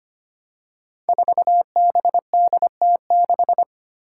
Morse code, heard as a keyed tone, about 25 wpm, 700 Hz. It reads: 4BDT6
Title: Five Challenging Letter-Number Combinations 25wpm